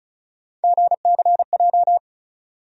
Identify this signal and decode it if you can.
Morse code, heard as a keyed tone, about 36 wpm, 700 Hz. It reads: GCJ
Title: USA Call Sign Postfixes - 3 Letters 35wpm